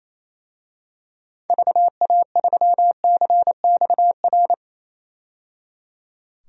Morse code, heard as a keyed tone, about 28 wpm, 700 Hz. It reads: VA3CXR
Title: DXCC-100 Call Signs - 6 Characters - 28wpm